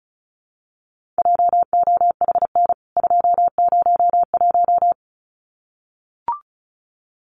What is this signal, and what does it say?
Morse code, heard as a keyed tone, about 34 wpm, 700 Hz. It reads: JOHN 201
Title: CWT Exchange 35wpm